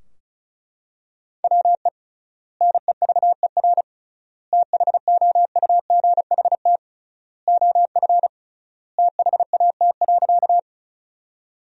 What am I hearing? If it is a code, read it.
Morse code, heard as a keyed tone, about 36 wpm, 700 Hz. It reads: WE NEVER THOUGHT OF THAT.
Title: 3x Word-Spacing - Sentences from Top 300 Words - 35wpm